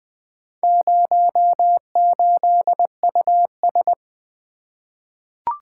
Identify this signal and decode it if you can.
Morse code, heard as a keyed tone, about 20 wpm, 700 Hz. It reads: 08US
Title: Four Letter-Number Combinations - 20wpm